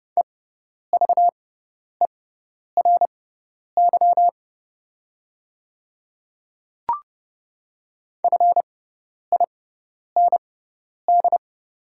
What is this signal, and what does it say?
Morse code, heard as a keyed tone, about 30 wpm, spaced as for 10 wpm, 700 Hz. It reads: EVERY FIND
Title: 6x Character Spacing - Top 200 Words - 30wpm